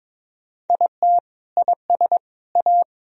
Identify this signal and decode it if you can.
Morse code, heard as a keyed tone, about 22 wpm, 700 Hz. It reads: IT IS A
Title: Sentences from Top 1000 Words - Review Part 1 - 22wpm